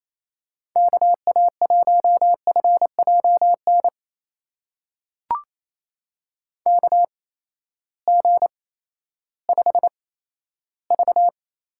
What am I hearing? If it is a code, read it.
Morse code, heard as a keyed tone, about 28 wpm, spaced as for 14 wpm, 700 Hz. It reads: KA1FJN KG5V
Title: USA Call Signs - 6 Characters - 8x Character Spacing - 28wpm